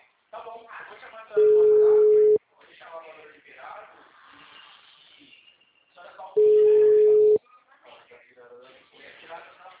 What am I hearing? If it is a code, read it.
Ringback tone (Europe, 425 Hz), with speech.
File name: out- -2032